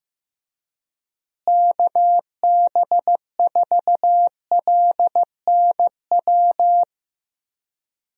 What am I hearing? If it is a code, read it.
Morse code, heard as a keyed tone, about 15 wpm, 700 Hz. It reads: KB4LNW
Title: USA Call Signs - 6 Characters - Repeated 2x - 15wpm